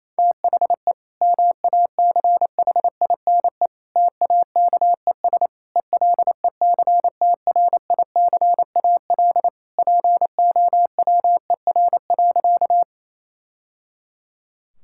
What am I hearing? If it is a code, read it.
Morse code, heard as a keyed tone, about 28 wpm, 700 Hz. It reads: THE MACHINE TAKES ELECTRICAL POWER.
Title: Sentences from Top 1000 Words - Review Part 6 - 28wpm